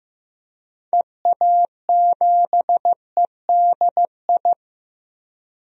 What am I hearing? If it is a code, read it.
Morse code, heard as a keyed tone, about 15 wpm, 700 Hz. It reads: EA7EDI
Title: DXCC-100 Call Signs - 6 Characters - Repeated 2x - 15wpm